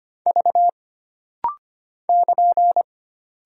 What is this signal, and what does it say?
Morse code, heard as a keyed tone, about 25 wpm, 700 Hz. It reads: V <KN>
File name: Single Character (Letter:Number:Punctuation:Prosign) - Rapid-Fire - 25wpm